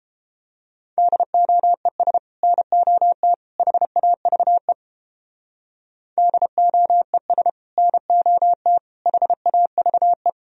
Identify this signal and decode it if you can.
Morse code, heard as a keyed tone, about 29 wpm, 700 Hz. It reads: DOES NOT HAVE DOES NOT HAVE